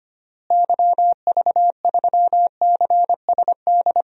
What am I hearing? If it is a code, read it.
Morse code, heard as a keyed tone, about 25 wpm, 700 Hz. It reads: YV3CSD
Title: DXCC-100 Call Signs - 6 Characters - Repeated 2x - 25wpm